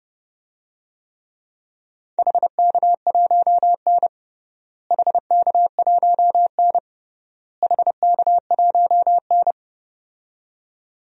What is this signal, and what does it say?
Morse code, heard as a keyed tone, about 30 wpm, 700 Hz. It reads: HK1N HK1N HK1N
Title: DXCC-100 Call Signs - 4 Characters - Repeated 3x - 30wpm